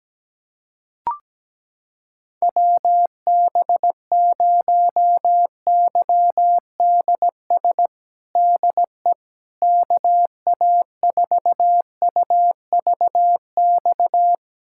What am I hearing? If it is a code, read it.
Morse code, heard as a keyed tone, about 17 wpm, 700 Hz. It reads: WB0YDS DE KA4UVX
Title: USA Callsign DE Callsign - 6 Characters - 17wpm